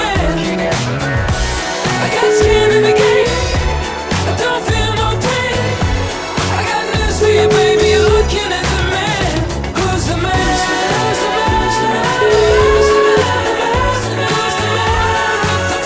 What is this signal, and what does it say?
Ringback tone (Europe, 425 Hz), with noise.